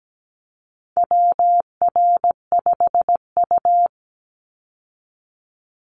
Morse code, heard as a keyed tone, about 17 wpm, 700 Hz. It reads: WR5U
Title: Call Signs - 17wpm